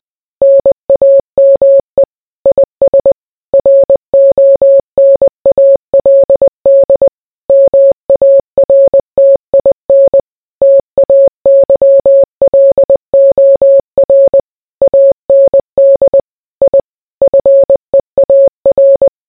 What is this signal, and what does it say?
Morse code, heard as a keyed tone, about 20 wpm, 550 Hz. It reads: NAME IS RONALD MARTIN TAYLOR AND I FEAR